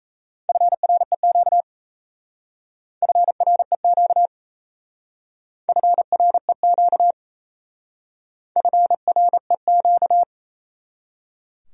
Morse code, heard as a keyed tone, about 34 wpm, 700 Hz. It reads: FREQ FREQ FREQ FREQ
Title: Speed-Racing Abbreviations - 28wpm